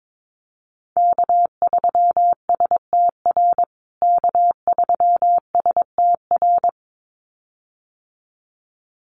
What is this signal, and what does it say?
Morse code, heard as a keyed tone, about 22 wpm, 700 Hz. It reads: K3STR K3STR
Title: USA Call Signs Repeated 2x 22wpm